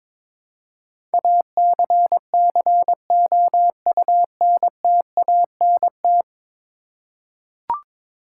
Morse code, heard as a keyed tone, about 22 wpm, 700 Hz. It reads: ACCOUNTANT
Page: morse-code-ninja-practice-22wpm